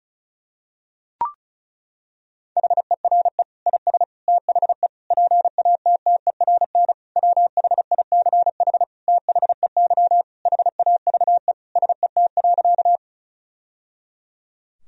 Morse code, heard as a keyed tone, about 34 wpm, 700 Hz. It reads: HERE IS THE PATTERN WHICH THEY HAVE SET.
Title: Sentences from Top 1000 Words - Review Part 2 - 35wpm